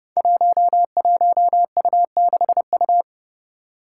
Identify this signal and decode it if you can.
Morse code, heard as a keyed tone, about 30 wpm, 700 Hz. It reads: 11U6U
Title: Five Challenging Letter-Number Combinations 30wpm